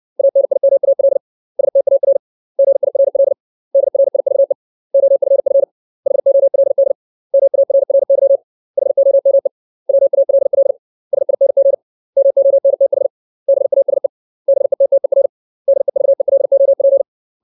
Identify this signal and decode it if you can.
Morse code, heard as a keyed tone, about 60 wpm, 550 Hz. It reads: ANIMAL STAR GIRL DRIVE OFF HOLD MARRY HOME WALL SING MONTH BASE BETTER DEVELOP
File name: 632 words / 632 Words 60WPM